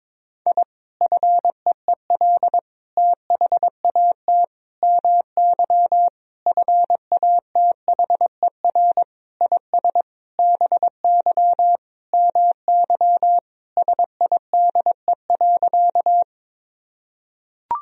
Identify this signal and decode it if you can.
Morse code, heard as a keyed tone, about 22 wpm, 700 Hz. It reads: I FEEL THAT MY FATHER IS BY MY SIDE.
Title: Sentences from Top 1000 Words - Review Part 1 - 22wpm